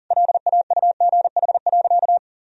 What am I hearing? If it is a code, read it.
Morse code, heard as a keyed tone, about 40 wpm, 700 Hz. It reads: LAUGH.
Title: Sentences from Top 1000 Words - Review Part 1 - 40wpm